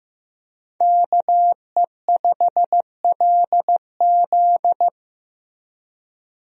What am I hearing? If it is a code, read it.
Morse code, heard as a keyed tone, about 15 wpm, 700 Hz. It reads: KE5LZ
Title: USA Call Signs - 5 Characters - Repeated 3x - 15wpm